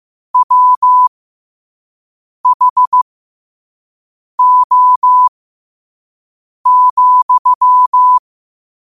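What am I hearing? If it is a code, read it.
Morse code, heard as a keyed tone, about 15 wpm, spaced as for 5 wpm, 1000 Hz. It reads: WHO,
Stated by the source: Quote-of-the-day Morse code practice at 5 wpm.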